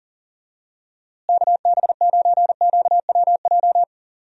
Morse code, heard as a keyed tone, about 40 wpm, 700 Hz. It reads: KB9QWJ
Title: USA Call Signs - 6 Characters - Repeated 2x - 40wpm